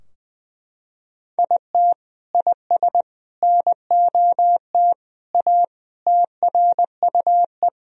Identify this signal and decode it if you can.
Morse code, heard as a keyed tone, about 20 wpm, 700 Hz. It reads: IT IS NOT A TRUE
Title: Sentences from Top 1000 Words - Review Part 4 - 20wpm